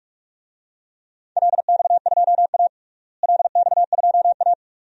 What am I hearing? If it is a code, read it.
Morse code, heard as a keyed tone, about 42 wpm, 700 Hz. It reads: LX2A LX2A
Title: DXCC-100 Call Signs - 4 Characters - Repeated 2x - 45wpm